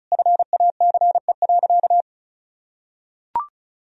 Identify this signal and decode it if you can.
Morse code, heard as a keyed tone, about 34 wpm, 700 Hz. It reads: FACE.
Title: Sentences from Top 1000 Words - Review Part 2 - 35wpm